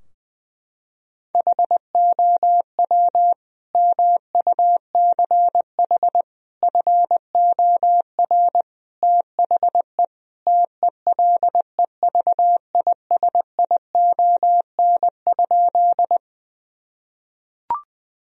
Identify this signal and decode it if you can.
Morse code, heard as a keyed tone, about 21 wpm, 700 Hz. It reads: HOW MUCH FOR THE TELEVISION?
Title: Sentences from Top 1000 Words - Review Part 4 - 20wpm